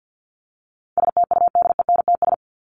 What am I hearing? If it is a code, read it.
Morse code, heard as a keyed tone, about 50 wpm, 700 Hz. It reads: STUDENTS
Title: ES:S Suffix - Top 100 Words with Base Words 50wpm